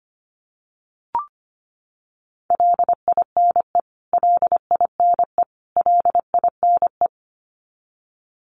Morse code, heard as a keyed tone, about 25 wpm, 700 Hz. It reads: LINE LINE LINE
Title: Top 200 Words Repeated 25wpm